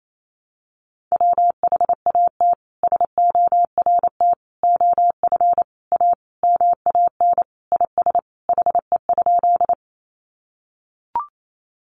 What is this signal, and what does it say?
Morse code, heard as a keyed tone, about 28 wpm, 700 Hz. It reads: WHAT SORT OF A MAN IS HE?
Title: Sentences from Top 1300 Words 28wpm